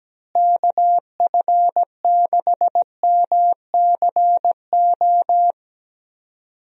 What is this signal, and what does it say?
Morse code, heard as a keyed tone, about 17 wpm, 700 Hz. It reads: KF6MCO
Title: USA Call Signs - 6 Characters - Repeated 3x - 17wpm